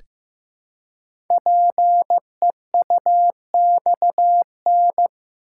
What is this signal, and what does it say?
Morse code, heard as a keyed tone, about 15 wpm, 700 Hz. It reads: PEUXN
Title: Code Groups - 5 Letters - 15wpm